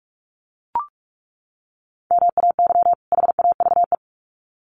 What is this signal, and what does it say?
Morse code, heard as a keyed tone, about 44 wpm, 700 Hz. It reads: MAY HAVE
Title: Sets of 2 Words 45wpm